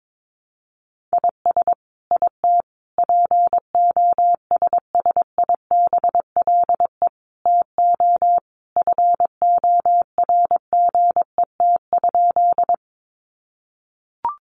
Morse code, heard as a keyed tone, about 22 wpm, 700 Hz. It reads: IS IT POSSIBLE TO FORGET?
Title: Sentences from Top 1200 Words 22wpm